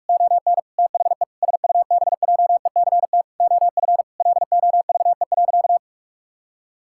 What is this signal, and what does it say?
Morse code, heard as a keyed tone, about 42 wpm, 700 Hz. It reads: ON THE SUBJECT OF LOVE.
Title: Sentences from Top 1600 Words 45wpm